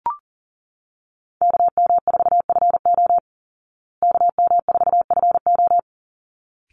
Morse code, heard as a keyed tone, about 40 wpm, 700 Hz. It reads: KM4FO KM4FO
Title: Call Signs Repeated 3x - 40wpm